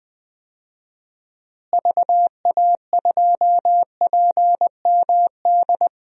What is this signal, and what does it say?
Morse code, heard as a keyed tone, about 20 wpm, 700 Hz. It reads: VA2PMD
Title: DXCC-100 Call Signs - 6 Characters - Repeated 3x - 20wpm